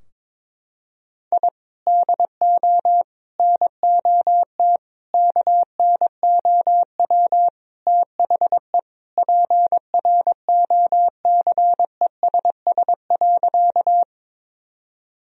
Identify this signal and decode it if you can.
Morse code, heard as a keyed tone, about 22 wpm, 700 Hz. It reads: I DO NOT KNOW THE PROCESS.
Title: Sentences from Top 1000 Words - Review Part 3 - 22wpm